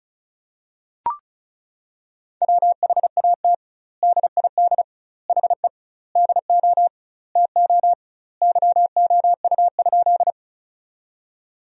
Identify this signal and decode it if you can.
Morse code, heard as a keyed tone, about 36 wpm, 700 Hz. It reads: WHAT DID HE DO TO YOU?
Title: 2x Word-Spacing - Sentences from Top 100 Words - 35wpm